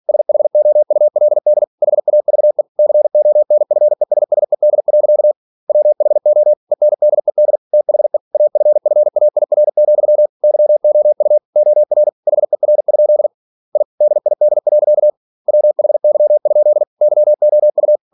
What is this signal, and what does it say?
Morse code, heard as a keyed tone, about 46 wpm, 600 Hz. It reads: SHOULD HAVE CONFESSED. WHO ENDED THE AFFAIR, YOU OR HER? I DID. WHY? YOU
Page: Never 47wpm